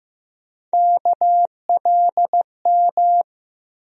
Morse code, heard as a keyed tone, about 15 wpm, 700 Hz. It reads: KLM
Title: USA Call Sign Postfixes - 3 Letters 15wpm